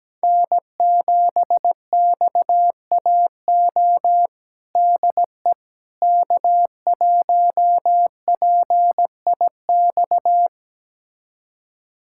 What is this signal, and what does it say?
Morse code, heard as a keyed tone, about 17 wpm, 700 Hz. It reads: N7XAO DE K1PIX
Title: USA Callsign DE Callsign - 5 Characters - 17wpm